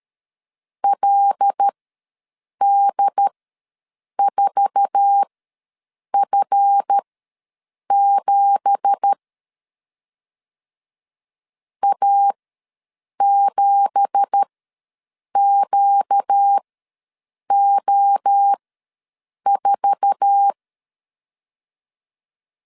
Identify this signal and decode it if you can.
Morse code, heard as a keyed tone, about 12 wpm, spaced as for 7 wpm, 800 Hz. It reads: LD4F7 A7QO4
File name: LEZIONI / 9 2867 ws 12wpm 3431